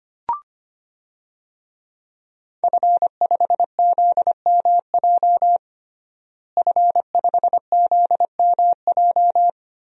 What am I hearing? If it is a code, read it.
Morse code, heard as a keyed tone, about 25 wpm, 700 Hz. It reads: F5ZMJ F5ZMJ
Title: DXCC-100 Call Signs - 5 Characters - Repeated 2x - 25wpm